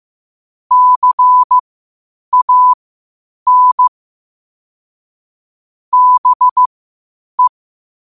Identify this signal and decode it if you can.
Morse code, heard as a keyed tone, about 15 wpm, spaced as for 8 wpm, 1000 Hz. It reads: CAN BE
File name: 8wpm